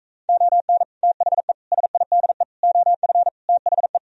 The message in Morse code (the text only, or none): ON THE SIDE OF THE